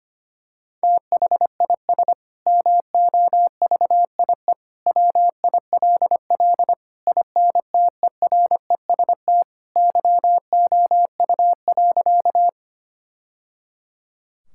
Morse code, heard as a keyed tone, about 25 wpm, 700 Hz. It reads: THIS MOVIE WILL INTEREST YOU.
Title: Sentences from Top 1000 Words - Review Part 5 - 25wpm